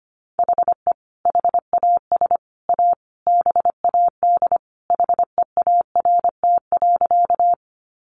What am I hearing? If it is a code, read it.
Morse code, heard as a keyed tone, about 25 wpm, 700 Hz. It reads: HE HAS A BAD HEART.
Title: Sentences from Top 1200 Words 25wpm